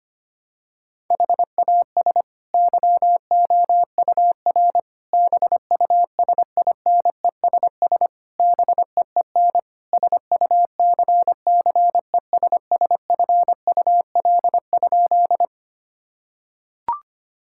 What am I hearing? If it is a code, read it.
Morse code, heard as a keyed tone, about 25 wpm, 700 Hz. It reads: HAS YOUR BUSINESS BEEN SUCCESSFUL?
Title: Sentences from Top 1000 Words - Review Part 6 - 25wpm